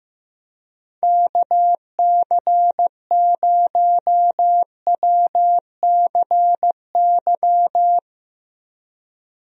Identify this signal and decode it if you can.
Morse code, heard as a keyed tone, about 15 wpm, 700 Hz. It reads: KC0WCY
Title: USA Call Signs - 6 Characters - Repeated 3x - 15wpm